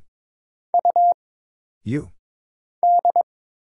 Morse code, heard as a keyed tone, about 22 wpm, 700 Hz. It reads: U D
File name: Single Character (Letter:Number:Punctuation:Prosign) - Mind-Melt - 22wpm